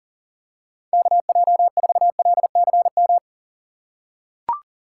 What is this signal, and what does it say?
Morse code, heard as a keyed tone, about 40 wpm, 700 Hz. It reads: KJ4LCM
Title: USA Call Signs - 6 Characters - 3x Character Spacing - 40wpm